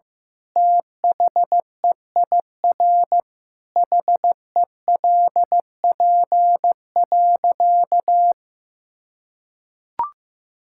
Morse code, heard as a keyed tone, about 16 wpm, 700 Hz. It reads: THEIR HELP.
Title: Sentences from Top 1000 Words - Review Part 6 - 15wpm